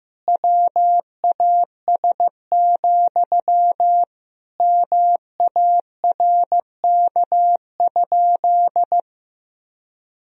Morse code, heard as a keyed tone, about 15 wpm, 700 Hz. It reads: WAS, MARK?
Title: Sentences from Top 1000 Words - Review Part 1 - 15wpm